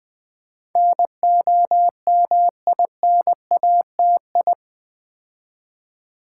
Morse code, heard as a keyed tone, about 20 wpm, 700 Hz. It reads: NOMINATI
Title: Build A Word - 4 Letters At A Time - 10-Letter Words - 20wpm